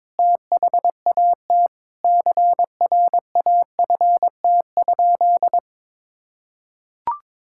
Morse code, heard as a keyed tone, about 22 wpm, 700 Hz. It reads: THAT CRAFT?
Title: Sentences from Top 1000 Words - Review Part 6 - 22wpm